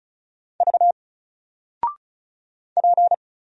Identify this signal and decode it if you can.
Morse code, heard as a keyed tone, about 36 wpm, 700 Hz. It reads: V P
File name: Single-Letter-Number-35wpm